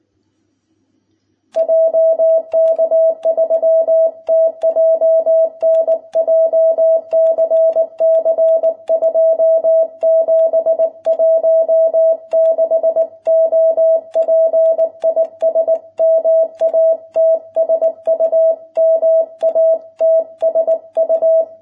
Morse code, heard as a keyed tone, about 19 wpm, 660 Hz. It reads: JK3TJN JCC 2716 OP IS MATSU MATSU